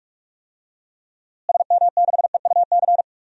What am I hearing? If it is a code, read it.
Morse code, heard as a keyed tone, about 41 wpm, 700 Hz. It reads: SM6EUC
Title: DXCC-100 Call Signs - 6 Characters - Repeated 2x - 45wpm